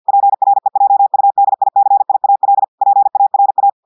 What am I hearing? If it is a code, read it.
Morse code, heard as a keyed tone, about 50 wpm, 800 Hz. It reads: PREJUDICIAL PARA
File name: audios / portugais / portugais texte en clair 1 vitesse 50 mots minutes